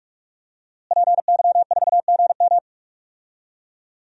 Morse code, heard as a keyed tone, about 46 wpm, 700 Hz. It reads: PY4GM
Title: DXCC-100 Call Signs - 5 Characters - Repeated 3x - 45wpm